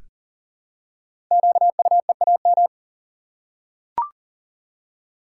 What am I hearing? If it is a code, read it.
Morse code, heard as a keyed tone, about 42 wpm, 700 Hz. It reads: QUEAM
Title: Code Groups - 5 Letters - 40wpm